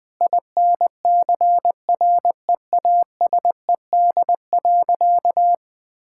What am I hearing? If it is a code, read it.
Morse code, heard as a keyed tone, about 20 wpm, 700 Hz. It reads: INCREASED.
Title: Sentences with 7-Word Phrases - Spoken Beginning - 20wpm